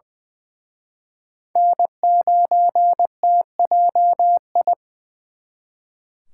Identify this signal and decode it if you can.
Morse code, heard as a keyed tone, about 23 wpm, 700 Hz. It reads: N9TJI
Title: Speed Racing - USA Call Signs 20wpm